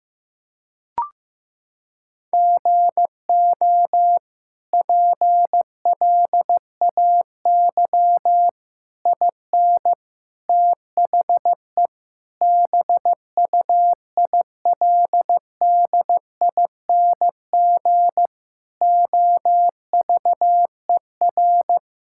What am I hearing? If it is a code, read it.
Morse code, heard as a keyed tone, about 16 wpm, 700 Hz. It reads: GO PLAY IN THE BUILDING OVER
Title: Sentences from Top 1000 Words - Review Part 5 - 15wpm